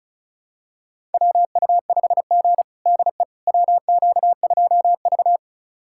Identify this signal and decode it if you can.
Morse code, heard as a keyed tone, about 36 wpm, 700 Hz. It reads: WU5G DE WQ2V
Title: USA Callsign DE Callsign - 4 Characters - 35wpm